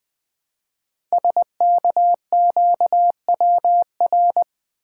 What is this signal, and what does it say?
Morse code, heard as a keyed tone, about 20 wpm, 700 Hz. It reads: SKQWR
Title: Code Groups - 5 Letter:Numbers - 20wpm